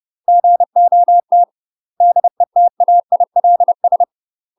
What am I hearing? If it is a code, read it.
Morse code, heard as a keyed tone, about 30 wpm, 700 Hz. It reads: GOT DETAILS
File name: UK MASTER QSOs TRAINING / 30WPM / UK MASTER QSOs 0033